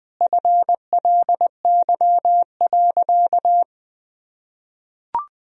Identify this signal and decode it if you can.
Morse code, heard as a keyed tone, about 20 wpm, 700 Hz. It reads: FLY.
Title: Sentences from Top 1000 Words - Review Part 2 - 20wpm